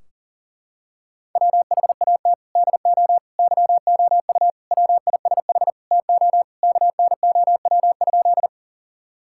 Morse code, heard as a keyed tone, about 40 wpm, 700 Hz. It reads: WHAT DO YOU WISH TO KNOW?
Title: Sentences from Top 1600 Words 40wpm